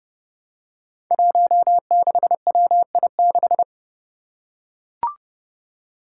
Morse code, heard as a keyed tone, about 31 wpm, 700 Hz. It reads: 16WI6
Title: Five Challenging Letter-Number Combinations 30wpm